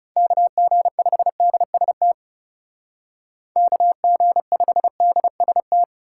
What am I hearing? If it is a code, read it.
Morse code, heard as a keyed tone, about 35 wpm, 700 Hz. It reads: KG5DST KG5DST
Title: Speed Racing - USA 2x3 Call Signs 30wpm